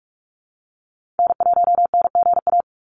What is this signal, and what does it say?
Morse code, heard as a keyed tone, about 46 wpm, 700 Hz. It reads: N1NGA
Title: USA Call Signs 45wpm